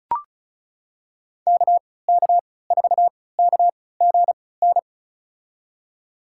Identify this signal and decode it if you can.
Morse code, heard as a keyed tone, about 36 wpm, spaced as for 20 wpm, 700 Hz. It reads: KK4KGN
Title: USA Call Signs - 3x Character Spacing - 35wpm